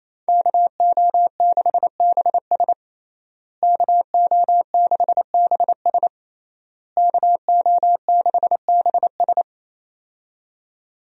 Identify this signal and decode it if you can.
Morse code, heard as a keyed tone, about 28 wpm, 700 Hz. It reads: KO6BS KO6BS KO6BS
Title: USA Call Signs - 5 Characters - Repeated 3x - 28wpm